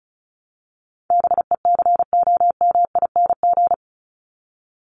Morse code, heard as a keyed tone, about 34 wpm, 700 Hz. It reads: BECOMING
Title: ING Suffix - Top 100 Words with Base Words 35wpm